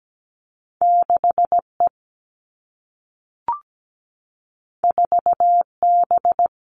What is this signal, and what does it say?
Morse code, heard as a keyed tone, about 17 wpm, 700 Hz. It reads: 6E 4B
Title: All Cut Numbers - 2-Digit 17wpm